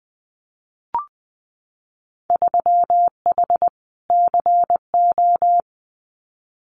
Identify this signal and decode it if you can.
Morse code, heard as a keyed tone, about 20 wpm, 700 Hz. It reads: 3H CO
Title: Winter Field Day Exchanges 20wpm